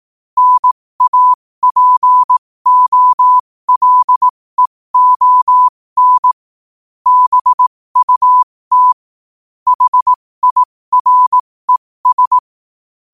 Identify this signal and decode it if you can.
Morse code, heard as a keyed tone, about 18 wpm, spaced as for 15 wpm, 1000 Hz. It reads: NAPOLEON BUT HIRES